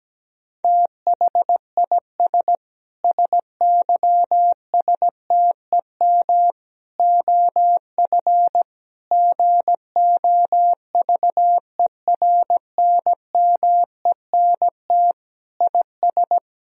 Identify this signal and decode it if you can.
Morse code, heard as a keyed tone, about 17 wpm, 700 Hz. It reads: THIS SYSTEM OF GOVERNMENT IS
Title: Sentences from Top 1000 Words - Review Part 3 - 17wpm